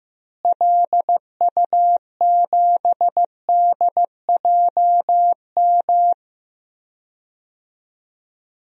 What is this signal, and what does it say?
Morse code, heard as a keyed tone, about 15 wpm, 700 Hz. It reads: LU7DJM
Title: DXCC-100 Call Signs - 6 Characters - Repeated 2x - 15wpm